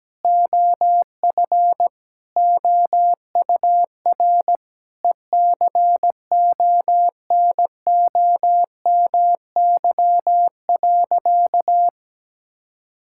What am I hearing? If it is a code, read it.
Morse code, heard as a keyed tone, about 17 wpm, 700 Hz. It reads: OF OUR ECONOMY.
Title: Sentences from Top 1000 Words - Review Part 5 - 17wpm